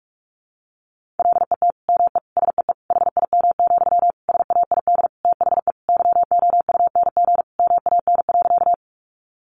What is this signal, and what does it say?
Morse code, heard as a keyed tone, about 44 wpm, 700 Hz. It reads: LET ME SEE HIM, SAID THE YOUNG MAN.
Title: Sentences from Top 400 Words 45wpm